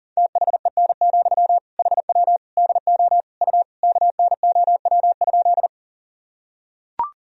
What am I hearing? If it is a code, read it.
Morse code, heard as a keyed tone, about 40 wpm, 700 Hz. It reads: THEN, HW DO U KNOW?
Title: Sentences from Top 100 Words and Abbreviations 40wpm